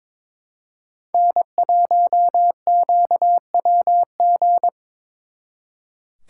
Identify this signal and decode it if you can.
Morse code, heard as a keyed tone, about 25 wpm, 700 Hz. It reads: N1QWG
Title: Speed Racing - USA 1x3 Call Signs 22wpm